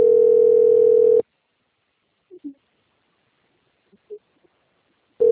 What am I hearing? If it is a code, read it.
Ringback tone (North America), 440 + 480 Hz.